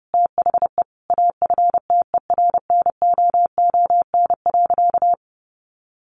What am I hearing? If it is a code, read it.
Morse code, heard as a keyed tone, about 30 wpm, 700 Hz. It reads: THE AFTERNOON.
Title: Sentences from Top 1500 Words 30wpm